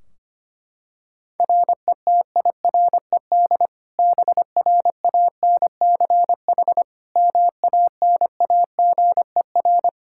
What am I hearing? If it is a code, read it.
Morse code, heard as a keyed tone, about 25 wpm, 700 Hz. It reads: RETIRED BRANCH MANAGER
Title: Practice QSO - Exchange 3 - More Info - 25wpm